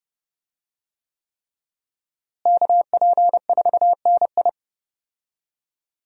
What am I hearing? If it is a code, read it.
Morse code, heard as a keyed tone, about 30 wpm, 700 Hz. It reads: KP4NI
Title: USA Call Signs - 5 Characters - Repeated 3x - 30wpm